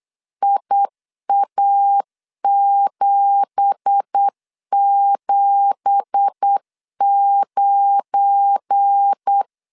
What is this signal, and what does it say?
Morse code, heard as a keyed tone, about 8 wpm, 800 Hz. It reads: IA779
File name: 8wpm